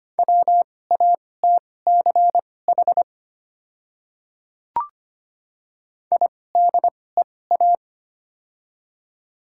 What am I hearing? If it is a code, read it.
Morse code, heard as a keyed tone, about 25 wpm, spaced as for 18 wpm, 700 Hz. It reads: WATCH IDEA
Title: 2x Character Spacing - Top 300 Words - 25wpm